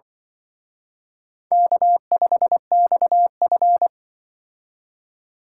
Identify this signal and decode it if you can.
Morse code, heard as a keyed tone, about 23 wpm, 700 Hz. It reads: K5XF
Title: Speed Racing - USA 1x2 Call Signs 20wpm